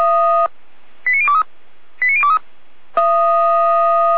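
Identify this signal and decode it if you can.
Five-tone selective call (ZVEI), heard as quick runs of five tones, tones of about 70 ms. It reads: 89023 89023
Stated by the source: ZVEI five-tone selective call received on SDR.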